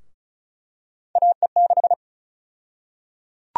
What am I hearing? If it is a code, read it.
Morse code, heard as a keyed tone, about 34 wpm, 700 Hz. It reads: AE6
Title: USA Call Sign Prefixes - 35wpm